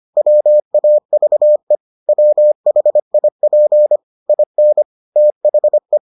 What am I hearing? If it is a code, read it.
Morse code, heard as a keyed tone, about 25 wpm, 600 Hz. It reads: WAVE WHIP IN THE